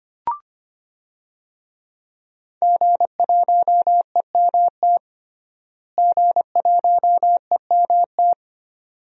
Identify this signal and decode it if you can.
Morse code, heard as a keyed tone, about 25 wpm, 700 Hz. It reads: G1EMT G1EMT
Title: DXCC-100 Call Signs - 5 Characters - Repeated 2x - 25wpm